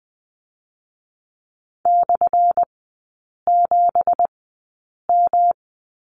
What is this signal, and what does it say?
Morse code, heard as a keyed tone, about 20 wpm, 700 Hz. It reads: / 7 M
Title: 178 Lesson 29e - Slash - Three Characters 20wpm